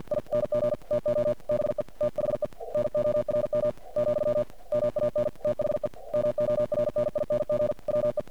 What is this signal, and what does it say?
Morse code, heard as a keyed tone, about 50 wpm, 600 Hz. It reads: ING TO BE THE NORM , MAN THE MORNING WE
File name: archived / CRSnet